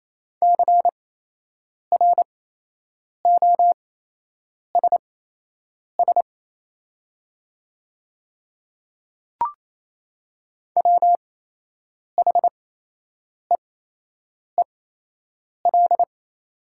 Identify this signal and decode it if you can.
Morse code, heard as a keyed tone, about 27 wpm, spaced as for 8 wpm, 700 Hz. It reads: CROSS WHEEL
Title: 8x Character Spacing - Top 500 Words - 28wpm